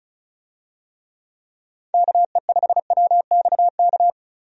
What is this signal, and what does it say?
Morse code, heard as a keyed tone, about 35 wpm, 700 Hz. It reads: KE5WXK
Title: USA Call Signs - 6 Characters - 35wpm